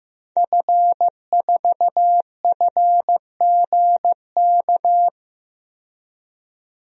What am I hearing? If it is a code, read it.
Morse code, heard as a keyed tone, about 16 wpm, 700 Hz. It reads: F4FGK
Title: DXCC-100 Call Signs - 5 Characters - 15wpm